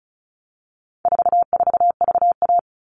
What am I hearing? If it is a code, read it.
Morse code, heard as a keyed tone, about 34 wpm, 700 Hz. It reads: 44VA
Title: All Cut Numbers - 4-Digit 35wpm